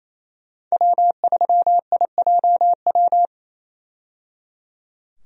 Morse code, heard as a keyed tone, about 28 wpm, 700 Hz. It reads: W3IJW
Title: USA Call Signs - 5 Characters - 28wpm